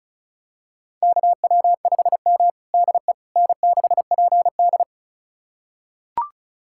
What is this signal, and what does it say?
Morse code, heard as a keyed tone, about 36 wpm, 700 Hz. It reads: KW5M DE N6PD
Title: USA Callsign DE Callsign - 4 Characters - 35wpm